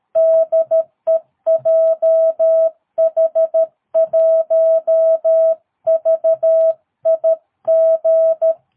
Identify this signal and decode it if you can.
Morse code, heard as a keyed tone, about 13 wpm, 650 Hz. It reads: DE JH1VIG
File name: jh8cbh / 2022cw